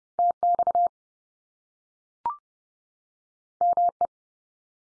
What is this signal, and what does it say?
Morse code, heard as a keyed tone, about 30 wpm, 700 Hz. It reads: TX ME
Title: US States 30WPM